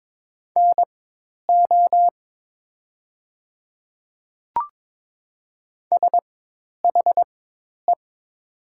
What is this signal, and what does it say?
Morse code, heard as a keyed tone, about 22 wpm, spaced as for 10 wpm, 700 Hz. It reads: NO SHE
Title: 4x Character Spacing - Top 100 Words - 22wpm